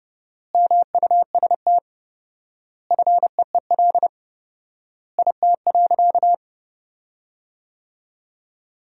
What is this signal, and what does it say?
Morse code, heard as a keyed tone, about 30 wpm, 700 Hz. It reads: MUST FEEL IT.
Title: 4x Word-Spacing - Sentences from Top 400 Words - 30wpm